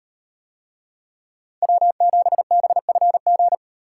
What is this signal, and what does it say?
Morse code, heard as a keyed tone, about 35 wpm, 700 Hz. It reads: W7BFG